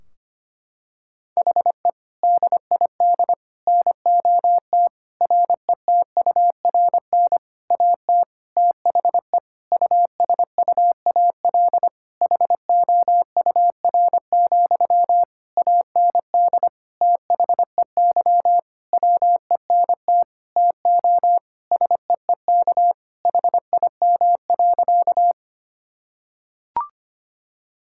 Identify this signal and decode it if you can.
Morse code, heard as a keyed tone, about 25 wpm, 700 Hz. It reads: HE DID NOT RETURN AT THE USUAL HOUR, AND THEY WENT TO SEEK HIM.
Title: Sentences from Top 1600 Words 25wpm